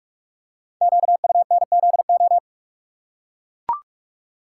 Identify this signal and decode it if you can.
Morse code, heard as a keyed tone, about 44 wpm, 690 Hz. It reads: QUNZO
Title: Code Groups - 5 Letters - 45wpm